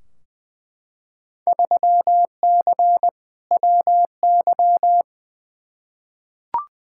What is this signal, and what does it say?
Morse code, heard as a keyed tone, about 20 wpm, 700 Hz. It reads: 3C WY
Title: Field Day Exchanges Encore - 20wpm